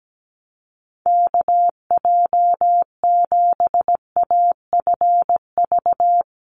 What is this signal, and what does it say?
Morse code, heard as a keyed tone, about 17 wpm, 700 Hz. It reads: KJ7AFV
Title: USA Call Signs 17wpm